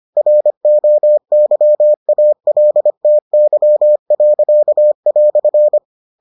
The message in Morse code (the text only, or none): ROYALTY."